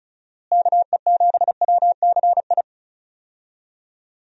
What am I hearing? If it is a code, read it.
Morse code, heard as a keyed tone, about 34 wpm, 700 Hz. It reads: KE7WCI
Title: USA Call Signs - 6 Characters - Repeated 3x - 35wpm